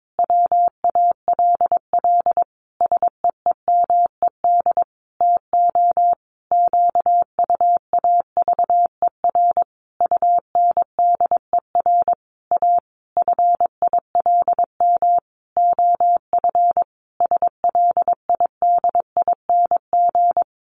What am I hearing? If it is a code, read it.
Morse code, heard as a keyed tone, about 22 wpm, 700 Hz. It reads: WALL SEEMED TO QUAVER UNDER A FILM OF SLIDING
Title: A Sound Of Thunder 22wpm